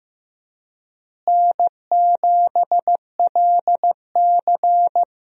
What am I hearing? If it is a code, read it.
Morse code, heard as a keyed tone, about 15 wpm, 700 Hz. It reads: N7LC
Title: USA Call Signs - 4 Characters - Repeated 2x - 15wpm